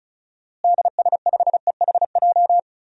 Morse code, heard as a keyed tone, about 34 wpm, 700 Hz. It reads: DS5EHJ
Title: DXCC-100 Call Signs - 6 Characters - Repeated 2x - 35wpm